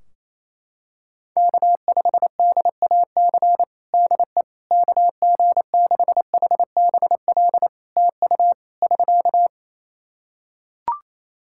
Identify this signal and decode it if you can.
Morse code, heard as a keyed tone, about 28 wpm, 700 Hz. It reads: K5DAC DE KG6HBL TU <SK>
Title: K1USN SST QSO Practice 28wpm